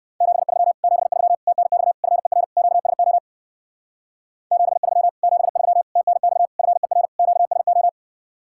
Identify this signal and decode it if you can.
Morse code, heard as a keyed tone, about 55 wpm, 700 Hz. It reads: 73 73 TNX FER QSO 73 73 TNX FER QSO
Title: Speed-Racing Common QSO Phrases - 45wpm